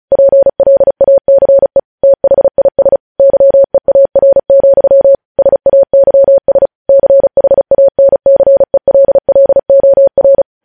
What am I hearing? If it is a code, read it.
Morse code, heard as a keyed tone, about 36 wpm, 550 Hz. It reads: PLACE THIS YEAR, SAYS CHANCELLOR
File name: podcast / News headlines-1x-35x35